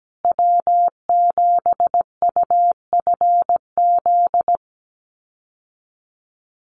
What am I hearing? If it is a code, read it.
Morse code, heard as a keyed tone, about 17 wpm, 700 Hz. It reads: W7UFZ
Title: USA Call Signs Repeated 3x 17wpm